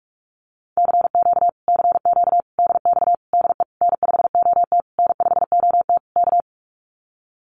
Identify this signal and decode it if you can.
Morse code, heard as a keyed tone, about 44 wpm, 700 Hz. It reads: CQ CQ DX DE N5OT N5OT K
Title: Common QSO Phrases Encore 45wpm